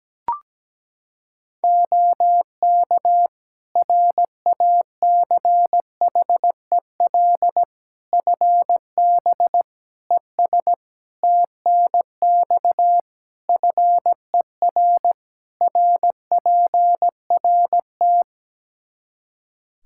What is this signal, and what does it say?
Morse code, heard as a keyed tone, about 17 wpm, 700 Hz. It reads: OK RACHEL FB ES TNX FER RPRT
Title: K1USN SST QSO Practice 17wpm